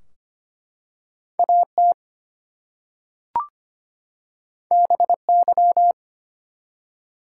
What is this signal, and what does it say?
Morse code, heard as a keyed tone, about 25 wpm, 700 Hz. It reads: AT BY
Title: Two Letter Words - 25wpm